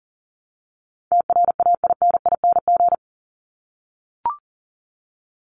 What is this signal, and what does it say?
Morse code, heard as a keyed tone, about 40 wpm, 700 Hz. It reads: TRAINING
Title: ING Suffix - Top 100 Words with Base Words 40wpm